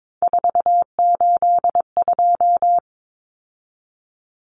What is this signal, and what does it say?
Morse code, heard as a keyed tone, about 22 wpm, 700 Hz. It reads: 482
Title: Common Cut Numbers - 3-Digit 22wpm